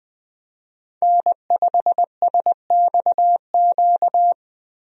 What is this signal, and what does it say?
Morse code, heard as a keyed tone, about 21 wpm, 700 Hz. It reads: N5SXQ
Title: USA Call Signs - 5 Characters - Repeated 2x - 20wpm